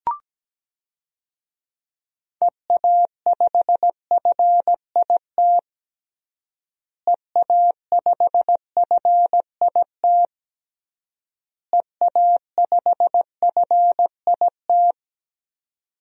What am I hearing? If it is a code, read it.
Morse code, heard as a keyed tone, about 17 wpm, 700 Hz. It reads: EA5FIT EA5FIT EA5FIT
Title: DXCC-100 Call Signs - 6 Characters - Repeated 3x - 17wpm